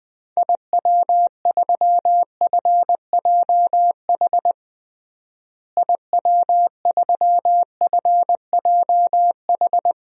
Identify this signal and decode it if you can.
Morse code, heard as a keyed tone, about 20 wpm, 700 Hz. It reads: IW3FJH IW3FJH
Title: DXCC-100 Call Signs - 6 Characters - Repeated 3x - 20wpm